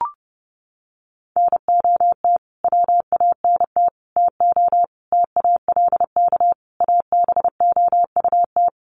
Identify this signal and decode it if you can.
Morse code, heard as a keyed tone, about 30 wpm, 700 Hz. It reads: NOT WANT TO TALK ABOUT
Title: Sets of 5 Words 30wpm